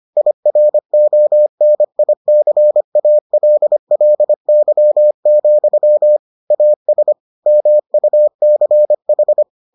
Morse code, heard as a keyed tone, about 25 wpm, 600 Hz. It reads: IRONICALLY, AS MUCH